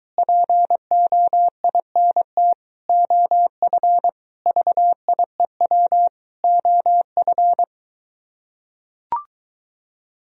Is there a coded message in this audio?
Yes: Morse code from 0.2 to 7.6 s, heard as a keyed tone, about 24 wpm, 700 Hz. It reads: POINT OF VIEW OF